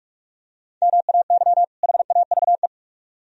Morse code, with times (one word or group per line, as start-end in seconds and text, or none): 0.8-1.6 MAY
1.8-2.7 HAVE